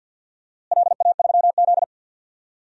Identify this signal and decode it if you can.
Morse code, heard as a keyed tone, about 53 wpm, 700 Hz. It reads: LA3Z
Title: DXCC-100 Call Signs - 4 Characters - Repeated 3x - 50wpm